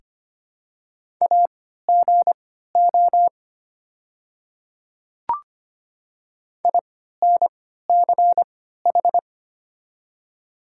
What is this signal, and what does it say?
Morse code, heard as a keyed tone, about 25 wpm, spaced as for 14 wpm, 700 Hz. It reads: AGO INCH
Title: 3x Character Spacing - Top 500 Words - 25wpm